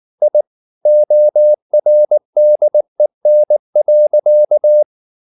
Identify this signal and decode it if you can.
Morse code, heard as a keyed tone, about 19 wpm, 600 Hz. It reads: I ORDEN.
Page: Ferie 19 wpm